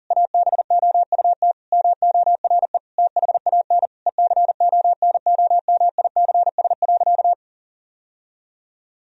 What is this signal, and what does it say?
Morse code, heard as a keyed tone, about 40 wpm, 700 Hz. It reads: ABOUT MORE THAN ECONOMICS.
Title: Sentences from Top 1000 Words - Review Part 3 - 40wpm